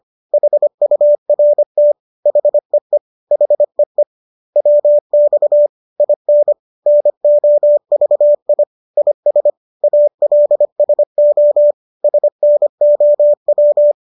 Morse code, heard as a keyed tone, about 25 wpm, 590 Hz. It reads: HURT HEE HEE WX IN NOVI IS ALSO SNOW
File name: qso221 25